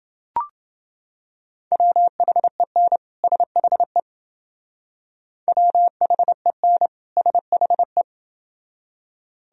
Morse code, heard as a keyed tone, about 29 wpm, 700 Hz. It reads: WHEN SHE WHEN SHE